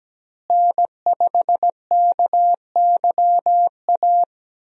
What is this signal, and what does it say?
Morse code, heard as a keyed tone, about 17 wpm, 700 Hz. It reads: N5KYA
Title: USA Call Signs - 5 Characters - Repeated 2x - 17wpm